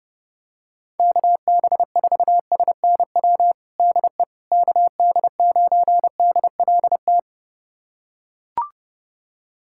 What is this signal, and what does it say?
Morse code, heard as a keyed tone, about 32 wpm, 700 Hz. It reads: KB4SNW DE KD9DLT
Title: USA Callsign DE Callsign 30wpm